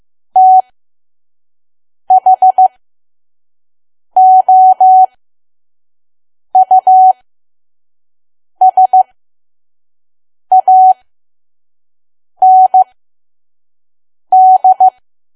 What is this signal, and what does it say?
Morse code, heard as a keyed tone, about 15 wpm, spaced as for 5 wpm, 750 Hz. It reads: THOUSAND
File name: Archive / 5 WPM files / PM